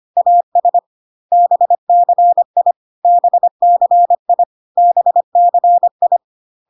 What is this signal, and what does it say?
Morse code, heard as a keyed tone, about 25 wpm, 700 Hz. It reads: AS BCI BCI BCI
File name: CW abbreviations alphabetical repeated 3 times 25 25wpm